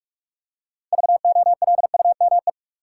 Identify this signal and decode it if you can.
Morse code, heard as a keyed tone, about 46 wpm, 700 Hz. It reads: VOLUME
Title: Six Letter Words - 45wpm